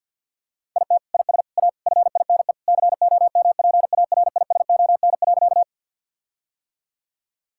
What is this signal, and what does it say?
Morse code, heard as a keyed tone, about 50 wpm, 700 Hz. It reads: IT IS A FINE COMPARISON.
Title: Sentences from Top 1600 Words 50wpm